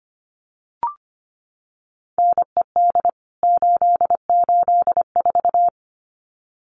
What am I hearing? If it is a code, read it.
Morse code, heard as a keyed tone, about 25 wpm, 700 Hz. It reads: NED 884
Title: CWT Exchange 25wpm